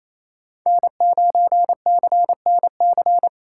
Morse code, heard as a keyed tone, about 27 wpm, 700 Hz. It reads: N9CNC
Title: USA Call Signs - 5 Characters - 2x Character Spacing - 28wpm